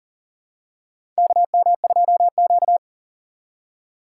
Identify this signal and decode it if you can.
Morse code, heard as a keyed tone, about 40 wpm, 700 Hz. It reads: KM2Q
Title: USA Call Signs - 4 Characters - 4x Character Spacing - 40wpm